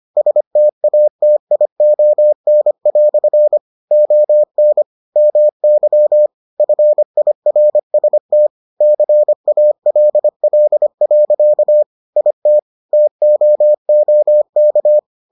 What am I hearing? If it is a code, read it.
Morse code, heard as a keyed tone, about 25 wpm, 600 Hz. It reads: STATION" ON MY FIRST CALL. IT TOOK